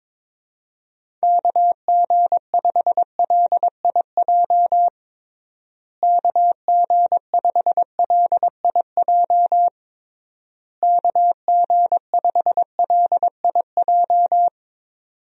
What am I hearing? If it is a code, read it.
Morse code, heard as a keyed tone, about 22 wpm, 700 Hz. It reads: KG5LIJ KG5LIJ KG5LIJ
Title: USA Call Signs - 6 Characters - Repeated 3x - 22wpm